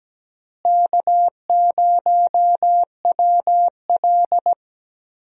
Morse code, heard as a keyed tone, about 17 wpm, 700 Hz. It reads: K0WL
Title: USA Call Signs - 4 Characters - Repeated 3x - 17wpm